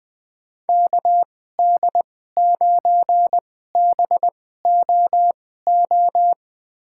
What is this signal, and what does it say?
Morse code, heard as a keyed tone, about 20 wpm, spaced as for 15 wpm, 700 Hz. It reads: KD9BOO
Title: USA Call Signs - 2x Character Spacing - 20wpm